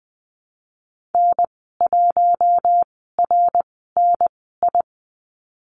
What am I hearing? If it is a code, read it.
Morse code, heard as a keyed tone, about 20 wpm, spaced as for 15 wpm, 700 Hz. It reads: N1RNI
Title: 173 Lesson 28g - Number 9 - Call Signs 20wpm